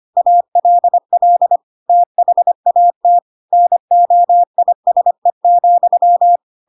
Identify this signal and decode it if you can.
Morse code, heard as a keyed tone, about 25 wpm, 700 Hz. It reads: ALL THAT NOISE,
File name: Pinocchio / Pinocchio ch3 25 25 ik1hge